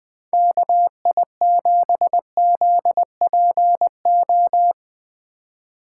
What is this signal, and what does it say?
Morse code, heard as a keyed tone, about 21 wpm, 700 Hz. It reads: KI7ZPO
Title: Speed Racing - USA Call Signs 20wpm